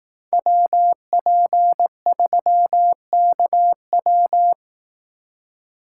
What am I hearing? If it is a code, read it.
Morse code, heard as a keyed tone, about 18 wpm, 700 Hz. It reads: WP3KW
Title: Speed Racing - USA 2x2 Call Signs 15wpm